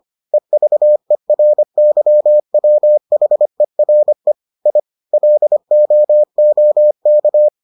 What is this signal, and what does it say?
Morse code, heard as a keyed tone, about 25 wpm, 600 Hz. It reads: EVERYWHERE I LOOK